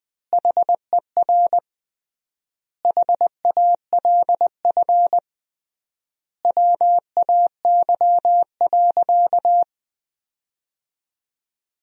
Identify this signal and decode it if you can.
Morse code, heard as a keyed tone, about 21 wpm, 700 Hz. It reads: HER HALF WAY.
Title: 3x Word-Spacing - Sentences from Top 400 Words - 20wpm